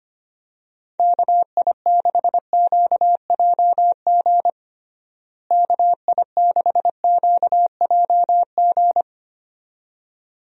Morse code, heard as a keyed tone, about 25 wpm, 700 Hz. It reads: KI6QJG KI6QJG
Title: USA Call Signs - 6 Characters - Repeated 2x - 25wpm